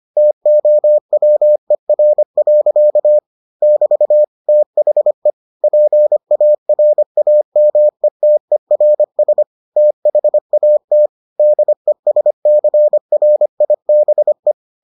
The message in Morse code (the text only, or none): TOWER. <BT> THE PARAMETERS THAT DESCRIBE